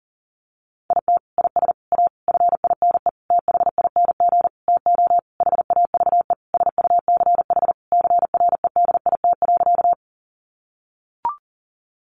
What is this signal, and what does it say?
Morse code, heard as a keyed tone, about 40 wpm, 700 Hz. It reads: IT IS A FINE THING TO HAVE SUCH CREDIT.
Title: Sentences from Top 1000 Words 40wpm